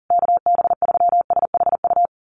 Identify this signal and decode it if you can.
Morse code, heard as a keyed tone, about 40 wpm, 700 Hz. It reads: KB3SHU
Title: USA Call Signs Repeated 3x 40wpm